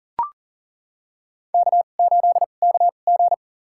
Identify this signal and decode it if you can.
Morse code, heard as a keyed tone, about 40 wpm, spaced as for 29 wpm, 700 Hz. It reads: K8KG
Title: USA Call Signs - 4 Characters - 2x Character Spacing - 40wpm